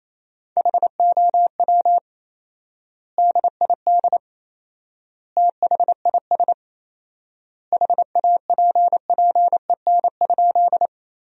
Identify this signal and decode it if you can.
Morse code, heard as a keyed tone, about 28 wpm, 700 Hz. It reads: HOW DID THIS HAPPEN?
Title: 4x Word-Spacing - Sentences from Top 400 Words - 28wpm